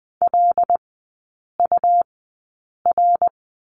Morse code, heard as a keyed tone, about 20 wpm, spaced as for 8 wpm, 700 Hz. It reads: LUR
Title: 127 Lesson 21e - Number 6 - Three Characters 20wpm